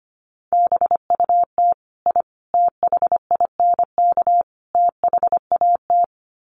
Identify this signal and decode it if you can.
Morse code, heard as a keyed tone, about 25 wpm, 700 Hz. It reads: BUT I THINK THAT
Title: Sets of 4 Words 25wpm